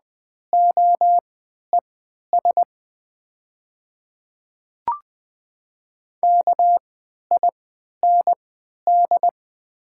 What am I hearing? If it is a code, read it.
Morse code, heard as a keyed tone, about 20 wpm, spaced as for 11 wpm, 700 Hz. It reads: OES KIND
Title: 3x Character Spacing - Top 200 Words - 20wpm